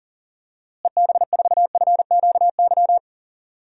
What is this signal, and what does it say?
Morse code, heard as a keyed tone, about 38 wpm, 700 Hz. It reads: EB4FQY
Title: DXCC-100 Call Signs - 6 Characters - Repeated 3x - 40wpm